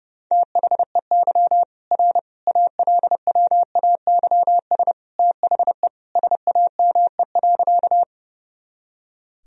Morse code, heard as a keyed tone, about 32 wpm, 700 Hz. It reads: THEY R ALWAYS THE SAME.
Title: Sentences from Top 300 Words and Abbreviations 30wpm